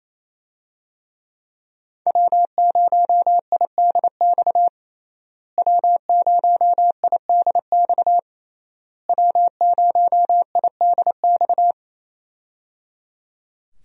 Morse code, heard as a keyed tone, about 28 wpm, 700 Hz. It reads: W0IDX W0IDX W0IDX
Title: USA Call Signs - 5 Characters - Repeated 3x - 28wpm